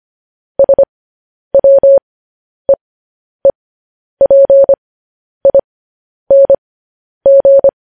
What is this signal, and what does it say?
Morse code, heard as a keyed tone, about 25 wpm, spaced as for 10 wpm, 550 Hz. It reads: SWEEPING
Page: news headlines in morse code at 10 wpm